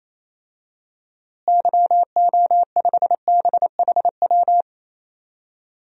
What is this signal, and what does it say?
Morse code, heard as a keyed tone, about 28 wpm, 700 Hz. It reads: YO5BHW
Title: DXCC-100 Call Signs - 6 Characters - Repeated 2x - 28wpm